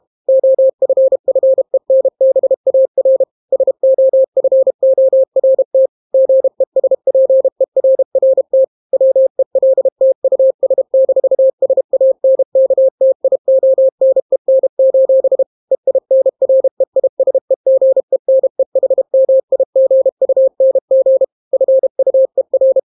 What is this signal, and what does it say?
Morse code, heard as a keyed tone, about 32 wpm, 530 Hz. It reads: OFFENBAR SOFORT GESPERRT WELT US-SANKTIONEN: EINREISEGENEHMIGUNG FUER